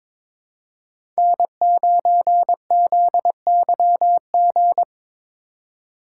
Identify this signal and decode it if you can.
Morse code, heard as a keyed tone, about 22 wpm, 700 Hz. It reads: N9ZYG
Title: USA Call Signs - 6x Character Spacing - 22wpm